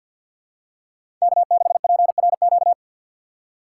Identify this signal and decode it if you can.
Morse code, heard as a keyed tone, about 53 wpm, 700 Hz. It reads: K6PRQ
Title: USA Call Signs - 5 Characters - 4x Character Spacing - 50wpm